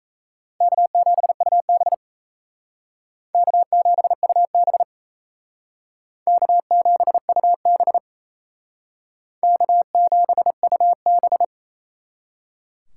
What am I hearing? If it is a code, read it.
Morse code, heard as a keyed tone, about 36 wpm, 700 Hz. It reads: K7UB K7UB K7UB K7UB